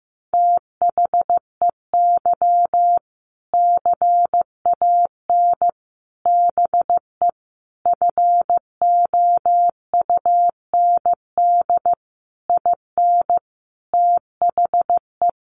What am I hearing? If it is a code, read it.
Morse code, heard as a keyed tone, about 15 wpm, 700 Hz. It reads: THEY CAN BE FOUND IN THE
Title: Sentences from Top 1500 Words 15wpm